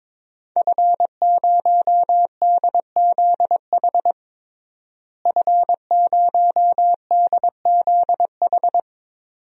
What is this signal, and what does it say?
Morse code, heard as a keyed tone, about 22 wpm, 700 Hz. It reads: F0DZH F0DZH
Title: DXCC-100 Call Signs - 5 Characters - Repeated 2x - 22wpm